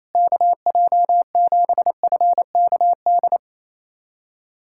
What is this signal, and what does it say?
Morse code, heard as a keyed tone, about 28 wpm, 700 Hz. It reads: KJ7FKD
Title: USA Callsign DE Callsign 28wpm